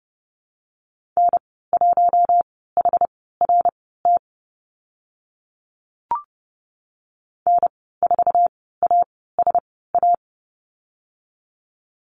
Morse code, heard as a keyed tone, about 31 wpm, spaced as for 17 wpm, 700 Hz. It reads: N1HRT N4ASA
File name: Lesson 14g - Number 2 - Call Signs 30wpm